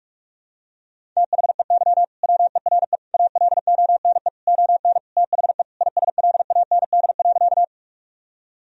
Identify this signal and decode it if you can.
Morse code, heard as a keyed tone, about 46 wpm, 700 Hz. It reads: THEY WERE ALONE ON THE ISLAND.
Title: Sentences from Top 1000 Words - Review Part 3 - 45wpm